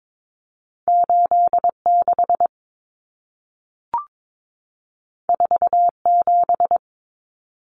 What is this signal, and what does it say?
Morse code, heard as a keyed tone, about 22 wpm, 700 Hz. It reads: 86 47
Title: Numbers - 2-Digit 22wpm